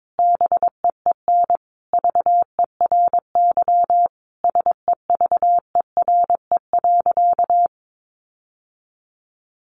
Morse code, heard as a keyed tone, about 22 wpm, 700 Hz. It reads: BEEN VERY SEVERE.
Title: Sentences from Top 1400 Words 22wpm